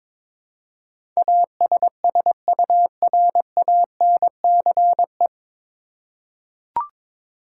Morse code, heard as a keyed tone, about 22 wpm, 700 Hz. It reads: ASSURANCE
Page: Morse Code Ninja practice 22wpm